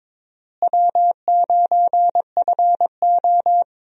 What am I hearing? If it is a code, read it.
Morse code, heard as a keyed tone, about 22 wpm, 700 Hz. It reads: W9FO
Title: USA Call Signs - 4 Characters - Repeated 3x - 22wpm